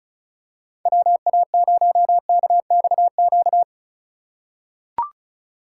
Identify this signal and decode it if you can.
Morse code, heard as a keyed tone, about 36 wpm, 700 Hz. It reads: WA0KXQ
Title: USA Call Signs - 4x Character Spacing - 35wpm